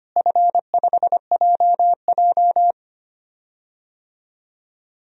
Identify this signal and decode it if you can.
Morse code, heard as a keyed tone, about 25 wpm, 700 Hz. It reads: F5JJ
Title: DXCC-100 Call Signs - 4 Characters - Repeated 3x - 25wpm